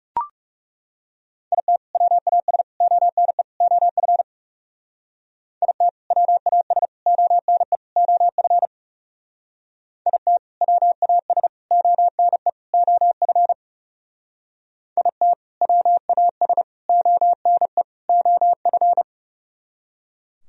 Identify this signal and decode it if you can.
Morse code, heard as a keyed tone, about 36 wpm, 700 Hz. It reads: IT WAS ONE OF IT WAS ONE OF IT WAS ONE OF IT WAS ONE OF
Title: Speed-Racing - Sets of 4 Words 30wpm